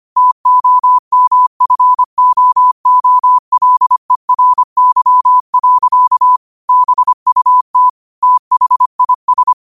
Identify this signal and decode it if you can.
Morse code, heard as a keyed tone, about 25 wpm, 1000 Hz. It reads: TOMFOOLERY. BUT THIS